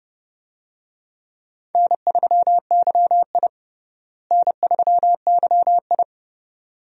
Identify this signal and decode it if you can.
Morse code, heard as a keyed tone, about 30 wpm, 700 Hz. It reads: N3YI N3YI
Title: USA Call Signs - 4 Characters - Repeated 2x - 30wpm